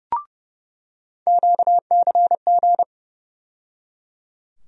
Morse code, heard as a keyed tone, about 30 wpm, 700 Hz. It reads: QCG
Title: USA Call Sign Postfixes - 3 Letters 30wpm